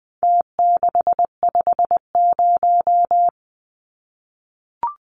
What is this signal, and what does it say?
Morse code, heard as a keyed tone, about 20 wpm, 700 Hz. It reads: T650
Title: Numbers - Contest Style - 4-Digit 20wpm